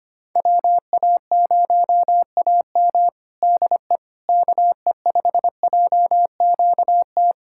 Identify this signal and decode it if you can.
Morse code, heard as a keyed tone, about 25 wpm, 700 Hz. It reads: WA0AM DE KE5JQT
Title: Practice QSO - Exchange 4 - Ending - 25wpm